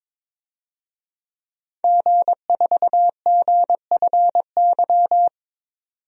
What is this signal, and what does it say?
Morse code, heard as a keyed tone, about 21 wpm, 700 Hz. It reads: G4GFY
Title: DXCC-100 Call Signs - 5 Characters - 22wpm